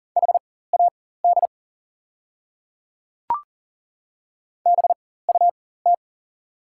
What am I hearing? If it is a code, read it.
Morse code, heard as a keyed tone, about 40 wpm, spaced as for 19 wpm, 700 Hz. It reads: HAD BUT
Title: 4x Character Spacing - Top 100 Words - 40wpm